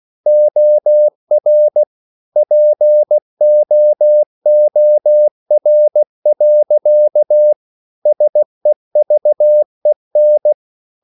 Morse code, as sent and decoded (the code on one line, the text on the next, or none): --- .-. / .--. --- --- .-. .-.-.- / ... ...- . -.
OR POOR. SEVEN